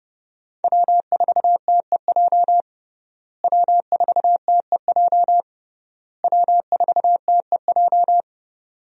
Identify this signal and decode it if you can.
Morse code, heard as a keyed tone, about 30 wpm, 700 Hz. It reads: W4TEJ W4TEJ W4TEJ
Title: USA Call Signs - 5 Characters - Repeated 3x - 30wpm